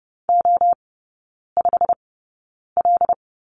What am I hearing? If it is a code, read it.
Morse code, heard as a keyed tone, about 30 wpm, spaced as for 9 wpm, 700 Hz. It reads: O5L
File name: course / 30:10wpm / 087 Lesson 15e - Number 5 - Three Characters 30wpm